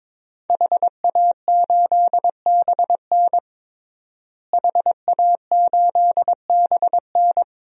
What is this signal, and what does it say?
Morse code, heard as a keyed tone, about 22 wpm, 700 Hz. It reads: HA8BN HA8BN
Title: DXCC-100 Call Signs - 5 Characters - Repeated 2x - 22wpm